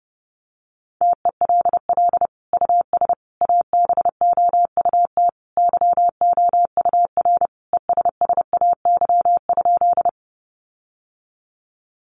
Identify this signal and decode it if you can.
Morse code, heard as a keyed tone, about 30 wpm, 700 Hz. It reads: TELL US ABOUT YOUR ESSAY?
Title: Sentences from Top 1500 Words 30wpm